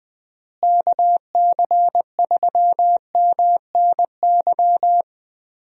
Morse code, heard as a keyed tone, about 21 wpm, 700 Hz. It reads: KC3MNY
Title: USA Call Signs - 6 Characters - Repeated 2x - 20wpm